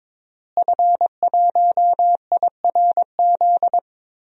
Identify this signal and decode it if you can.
Morse code, heard as a keyed tone, about 21 wpm, 700 Hz. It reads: F1IRZ
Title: DXCC-100 Call Signs - 5 Characters - Repeated 3x - 22wpm